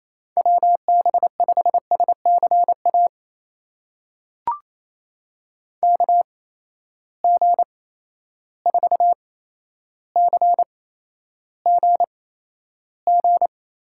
Morse code, heard as a keyed tone, about 28 wpm, spaced as for 12 wpm, 700 Hz. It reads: WB5SCA KG4CGG
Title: USA Call Signs - 6 Characters - 8x Character Spacing - 28wpm